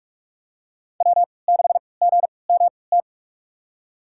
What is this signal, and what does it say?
Morse code, heard as a keyed tone, about 45 wpm, spaced as for 25 wpm, 700 Hz. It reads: W6GMT
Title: USA Call Signs - 5 Characters - 3x Character Spacing - 45wpm